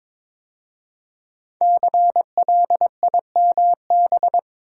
Morse code, heard as a keyed tone, about 22 wpm, 700 Hz. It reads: CLIMB
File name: 5 Letter Words - First 3 Letters Given - 22wpm